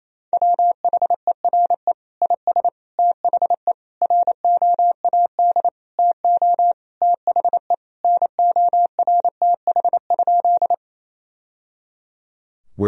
Morse code, heard as a keyed tone, about 27 wpm, 700 Hz. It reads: WHERE IS THE ROAD TO THE NORTH?
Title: Sentences from Top 1000 Words - Review Part 2 - 28wpm